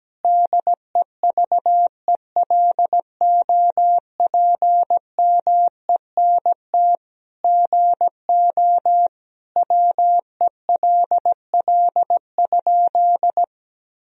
Morse code, heard as a keyed tone, about 17 wpm, 700 Hz. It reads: DEVELOPMENT GO WELL?
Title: Sentences from Top 1000 Words - Review Part 4 - 17wpm